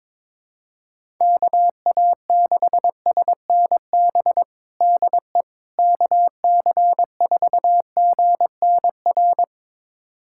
Morse code, heard as a keyed tone, about 22 wpm, 700 Hz. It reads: KA6SNB DE KC4GNR
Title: USA Callsign DE Callsign - 6 Characters - 22wpm